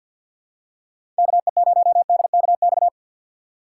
Morse code, heard as a keyed tone, about 51 wpm, 690 Hz. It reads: KE0DKX
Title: USA Call Signs - 6 Characters - 3x Character Spacing - 50wpm